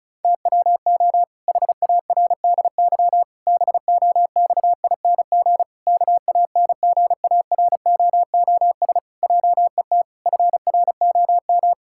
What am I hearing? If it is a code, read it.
Morse code, heard as a keyed tone, about 36 wpm, 700 Hz. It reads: TWO HARDY BOXING KANGAROOS JET FROM
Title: Pangram Challenge v2 - 35wpm